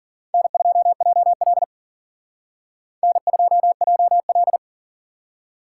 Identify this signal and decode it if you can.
Morse code, heard as a keyed tone, about 52 wpm, 700 Hz. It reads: N2JL N2JL